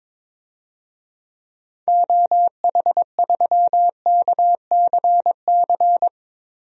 Morse code, heard as a keyed tone, about 22 wpm, 700 Hz. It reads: OH3KCC
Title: DXCC-100 Call Signs - 6 Characters - Repeated 3x - 22wpm